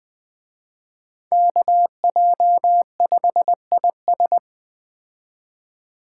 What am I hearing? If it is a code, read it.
Morse code, heard as a keyed tone, about 21 wpm, 700 Hz. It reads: KJ5IS
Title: Speed Racing - USA 2x2 Call Signs 17wpm